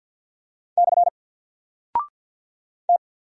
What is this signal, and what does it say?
Morse code, heard as a keyed tone, about 50 wpm, 700 Hz. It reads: / T
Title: Single Character (Letter:Number:Punctuation:Prosign) 50wpm